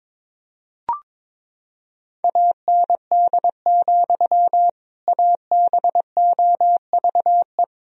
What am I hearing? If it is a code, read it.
Morse code, heard as a keyed tone, about 22 wpm, 700 Hz. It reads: AND, ABOVE
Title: Sentences from Top 1000 Words - Review Part 2 - 22wpm